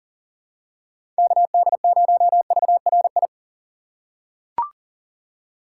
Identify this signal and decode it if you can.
Morse code, heard as a keyed tone, about 40 wpm, 700 Hz. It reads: KD0VRI
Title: USA Call Signs - 3x Character Spacing - 40wpm